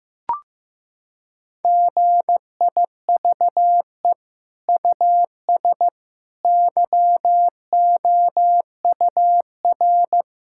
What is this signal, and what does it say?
Morse code, heard as a keyed tone, about 15 wpm, 700 Hz. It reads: GIVE US YOUR
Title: Sentences from Top 1000 Words - Review Part 5 - 15wpm